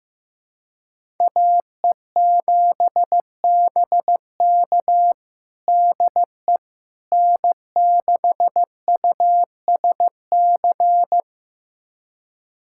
Morse code, heard as a keyed tone, about 16 wpm, 700 Hz. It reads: AE7BK DE N6USC
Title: USA Callsign DE Callsign - 5 Characters - 15wpm